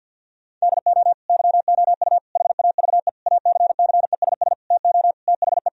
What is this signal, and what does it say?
Morse code, heard as a keyed tone, about 50 wpm, 700 Hz. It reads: DO YOU HAVE ACCESS TO THE
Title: Sentences from Top 1000 Words - Review Part 4 - 50wpm